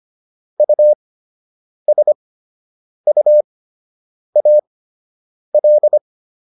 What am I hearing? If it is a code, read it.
Morse code, heard as a keyed tone, about 25 wpm, spaced as for 8 wpm, 600 Hz. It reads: USUAL